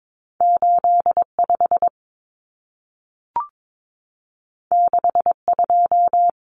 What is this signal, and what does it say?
Morse code, heard as a keyed tone, about 22 wpm, 700 Hz. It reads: 85 62
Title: Numbers - 2-Digit 22wpm